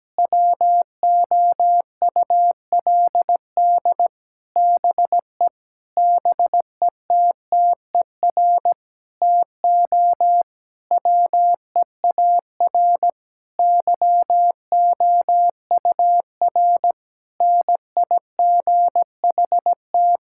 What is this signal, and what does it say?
Morse code, heard as a keyed tone, about 17 wpm, 700 Hz. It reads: WOULD BE BETTER TO WEAR YOUR NIGHT
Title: Sentences from Top 1600 Words 17wpm